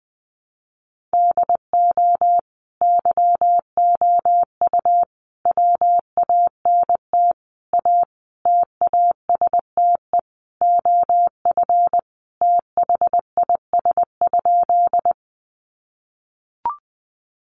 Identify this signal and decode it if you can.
Morse code, heard as a keyed tone, about 20 wpm, 700 Hz. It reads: DO YOU WANT A TASTE OF THIS?
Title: Sentences from Top 1100 Words 20wpm